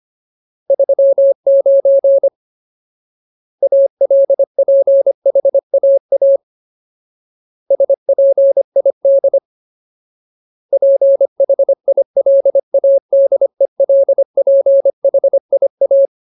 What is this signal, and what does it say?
Morse code, heard as a keyed tone, about 25 wpm, 550 Hz. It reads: 39 ALPHAA SPID PHILADELPHIA
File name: Ultimate Word List / 25WPM Ultimate